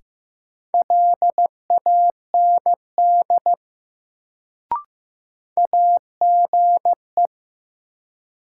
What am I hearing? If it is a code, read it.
Morse code, heard as a keyed tone, about 15 wpm, 700 Hz. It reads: LAND AGE
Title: Sentences from Top 500 Words - New Word Drill - 15wpm